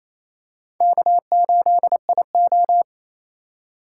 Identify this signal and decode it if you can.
Morse code, heard as a keyed tone, about 28 wpm, 700 Hz. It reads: K8IO
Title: USA Call Signs - 4 Characters - 4x Character Spacing - 28wpm